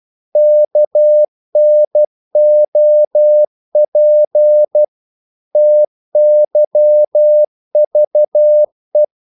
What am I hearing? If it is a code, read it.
Morse code, heard as a keyed tone, about 12 wpm, 600 Hz. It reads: KNOP TYVE
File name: Korte ord DK 12wpm